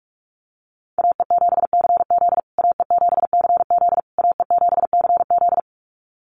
Morse code, heard as a keyed tone, about 44 wpm, 700 Hz. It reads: AE7CZ AE7CZ AE7CZ
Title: USA Call Signs Repeated 3x 45wpm